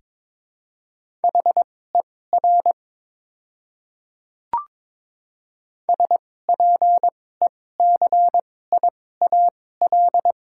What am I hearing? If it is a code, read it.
Morse code, heard as a keyed tone, about 22 wpm, spaced as for 16 wpm, 690 Hz. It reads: HER SPECIAL
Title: 2x Character Spacing - Top 500 Words - 22wpm